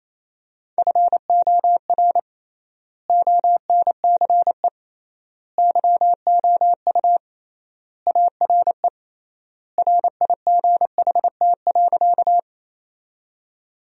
Morse code, heard as a keyed tone, about 28 wpm, 700 Hz. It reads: FOR ONCE YOU ARE RIGHT.
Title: 3x Word-Spacing - Sentences from Top 300 Words - 28wpm